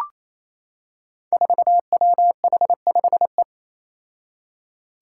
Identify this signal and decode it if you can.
Morse code, heard as a keyed tone, about 27 wpm, 700 Hz. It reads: 4WH5E
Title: Five Challenging Letter-Number Combinations 28wpm